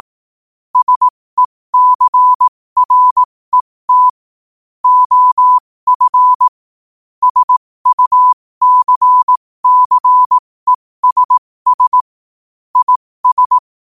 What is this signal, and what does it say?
Morse code, heard as a keyed tone, about 18 wpm, spaced as for 15 wpm, 1000 Hz. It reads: SECRET OF SUCCESS IS
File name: qotdfiles / 15wpm